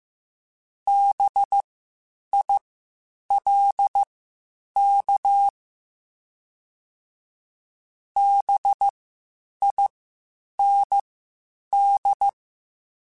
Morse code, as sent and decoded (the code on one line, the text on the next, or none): -... .. .-.. -.- / -... .. -. -..
BILK BIND